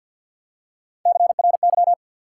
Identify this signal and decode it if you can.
Morse code, heard as a keyed tone, about 53 wpm, 700 Hz. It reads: CRY
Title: Three Letter Words - 50wpm